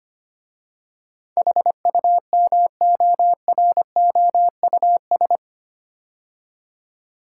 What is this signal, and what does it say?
Morse code, heard as a keyed tone, about 25 wpm, 700 Hz. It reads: HUMOROUS
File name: 8 Letter Words - First 3 Letters Given - 25wpm